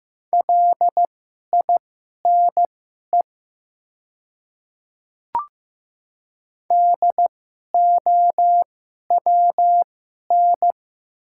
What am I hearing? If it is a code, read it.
Morse code, heard as a keyed tone, about 15 wpm, spaced as for 11 wpm, 700 Hz. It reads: LINE DOWN
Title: 2x Character Spacing - Top 200 Words - 15wpm